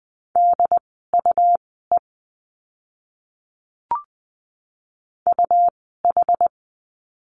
Morse code, heard as a keyed tone, about 20 wpm, spaced as for 14 wpm, 700 Hz. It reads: DUE UH
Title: 101 Lesson 17f - Letter u - Words 20wpm